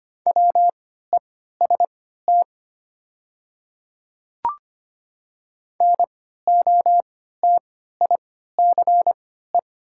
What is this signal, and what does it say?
Morse code, heard as a keyed tone, about 25 wpm, spaced as for 14 wpm, 700 Hz. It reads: WEST NOTICE
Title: 3x Character Spacing - Top 500 Words - 25wpm